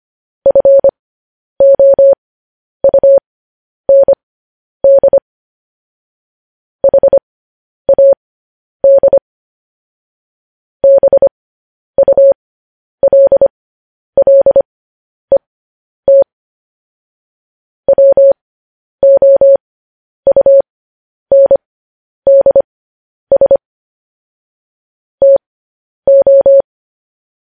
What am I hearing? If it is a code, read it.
Morse code, heard as a keyed tone, about 25 wpm, spaced as for 10 wpm, 550 Hz. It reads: FOUND HAD BULLET WOUNDS TO